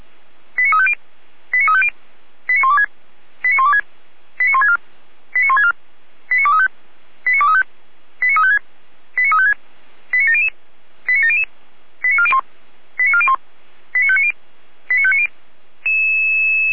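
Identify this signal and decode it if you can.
Five-tone selective call (ZVEI), heard as quick runs of five tones, tones of about 70 ms. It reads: 89350 89350 89126 89126 89164 89164 89235 89235 89356 89356 89700 89700 89401 89401 89590 89590
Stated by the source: ZVEI five-tone selective call 89080 received on SDR.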